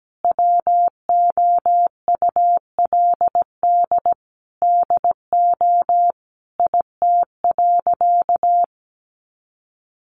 Morse code, heard as a keyed tone, about 17 wpm, 700 Hz. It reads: WOULD DO IT.
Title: Sentences from Top 1400 Words 17wpm